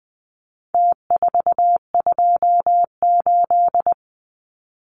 Morse code, heard as a keyed tone, about 20 wpm, 700 Hz. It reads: T428
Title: Numbers - Contest Style - 4-Digit 20wpm